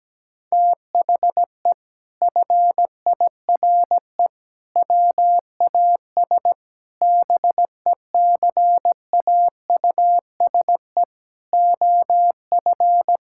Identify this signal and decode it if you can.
Morse code, heard as a keyed tone, about 17 wpm, 700 Hz. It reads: THE FIRE WAS BECAUSE OF
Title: Sentences from Top 1000 Words - Review Part 6 - 17wpm